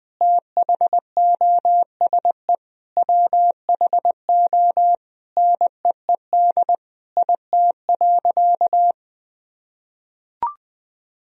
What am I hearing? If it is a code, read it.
Morse code, heard as a keyed tone, about 21 wpm, 700 Hz. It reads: THOSE WHO NEED IT.
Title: Sentences from Top 1000 Words - Review Part 4 - 20wpm